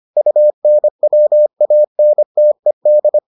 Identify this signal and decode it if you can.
Morse code, heard as a keyed tone, about 25 wpm, 600 Hz. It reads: UNWANTED